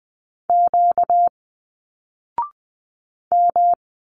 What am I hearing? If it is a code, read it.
Morse code, heard as a keyed tone, about 20 wpm, 700 Hz. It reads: Q M
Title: Single Letters 20wpm